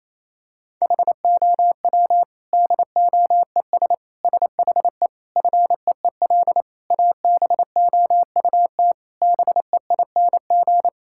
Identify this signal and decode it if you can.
Morse code, heard as a keyed tone, about 28 wpm, 700 Hz. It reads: HOW DOES SHE FEEL ABOUT BEING
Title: Sentences from Top 1000 Words - Review Part 6 - 28wpm